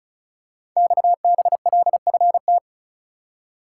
Morse code, heard as a keyed tone, about 36 wpm, 700 Hz. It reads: XBLFT
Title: Code Groups - 5 Letters - 35wpm